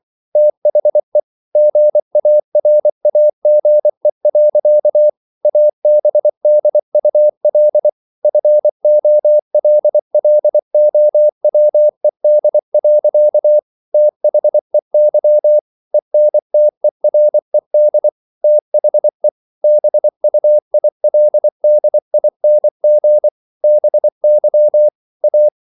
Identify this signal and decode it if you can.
Morse code, heard as a keyed tone, about 24 wpm, 600 Hz. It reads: THE GARAGE. ABDUL FOLLOWED. THEY ENTERED THE BUILDING BY A